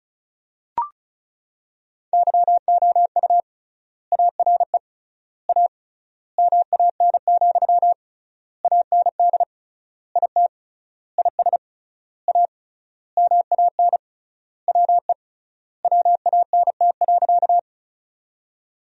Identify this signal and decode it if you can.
Morse code, heard as a keyed tone, about 36 wpm, 700 Hz. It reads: YOU ARE A MAN, AND IT IS A MAN WE WANT.
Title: 3x Word-Spacing - Sentences from Top 200 Words - 35wpm